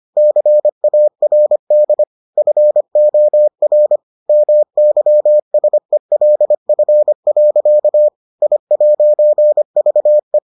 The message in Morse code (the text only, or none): CARD FOR MYSELF. I'VE